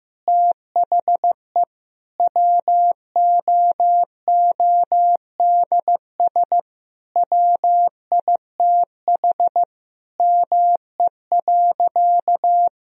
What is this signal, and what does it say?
Morse code, heard as a keyed tone, about 15 wpm, 700 Hz. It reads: THE WOODS WITH ME.
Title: Sentences from Top 1000 Words - Review Part 2 - 15wpm